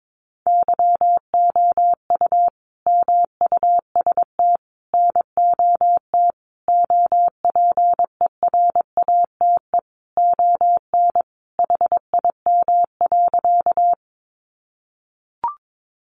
Morse code, heard as a keyed tone, about 22 wpm, 700 Hz. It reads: YOU MUST NOT OPERATE ON HIM.
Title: Sentences from Top 1500 Words 22wpm